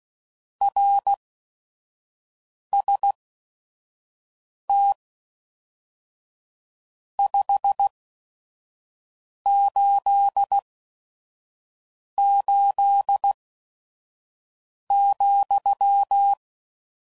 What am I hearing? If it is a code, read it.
Morse code, heard as a keyed tone, about 16 wpm, spaced as for 5 wpm, 800 Hz. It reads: RST 588,
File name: CW Code Groups @5WPM / RST's / Your RST's